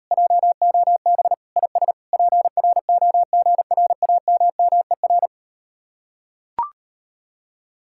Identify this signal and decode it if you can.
Morse code, heard as a keyed tone, about 38 wpm, 700 Hz. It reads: JOB IS PROGRAMMER